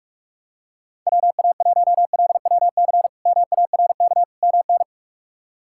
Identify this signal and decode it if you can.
Morse code, heard as a keyed tone, about 46 wpm, 700 Hz. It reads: WA1LWC MARK MN
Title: K1USN SST QSO Practice - 45wpm